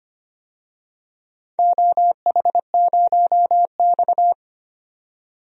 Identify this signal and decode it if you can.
Morse code, heard as a keyed tone, about 25 wpm, 700 Hz. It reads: OH0X
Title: DXCC-100 Call Signs - 4 Characters - 25wpm